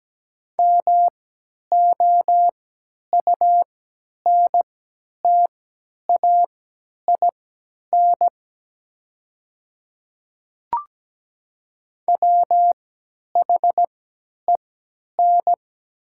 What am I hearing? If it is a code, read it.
Morse code, heard as a keyed tone, about 17 wpm, spaced as for 10 wpm, 700 Hz. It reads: MOUNTAIN WHEN
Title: 3x Character Spacing - Top 300 Words - 17wpm